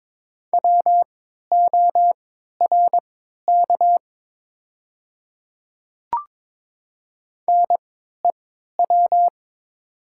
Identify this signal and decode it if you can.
Morse code, heard as a keyed tone, about 21 wpm, spaced as for 12 wpm, 700 Hz. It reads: WORK NEW
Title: 3x Character Spacing - Top 200 Words - 22wpm